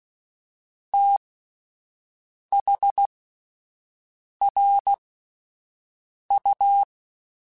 Morse code, heard as a keyed tone, about 16 wpm, spaced as for 5 wpm, 800 Hz. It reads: THRU
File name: (spaced) / QSOs Spaced-mp3s / WD0FBT 1 @10 sp